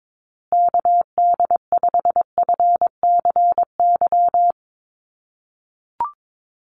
Morse code, heard as a keyed tone, about 22 wpm, 700 Hz. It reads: KD5FCY
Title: USA Call Signs Repeated 2x 22wpm